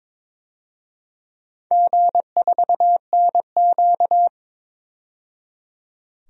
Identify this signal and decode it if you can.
Morse code, heard as a keyed tone, about 22 wpm, 700 Hz. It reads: G4NQ
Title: DXCC-100 Call Signs - 4 Characters - 22wpm